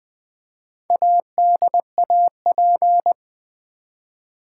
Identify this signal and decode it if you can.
Morse code, heard as a keyed tone, about 20 wpm, 700 Hz. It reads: ADAP
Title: Build A Word - 1 Letter At A Time - 5-Letter Words - 20wpm